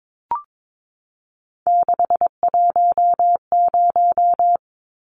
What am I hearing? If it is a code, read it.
Morse code, heard as a keyed tone, about 22 wpm, 700 Hz. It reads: 610
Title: Numbers - 3-Digit 22wpm